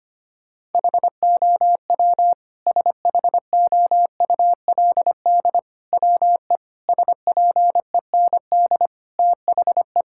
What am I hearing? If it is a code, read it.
Morse code, heard as a keyed tone, about 25 wpm, 700 Hz. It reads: HOW SHOULD WE SPEND THE
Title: Sentences from Top 1000 Words - Review Part 5 - 25wpm